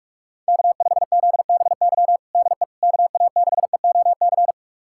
Morse code, heard as a keyed tone, about 42 wpm, 700 Hz. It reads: K5ZBY DE KA6EOC
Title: USA Callsign DE Callsign 45wpm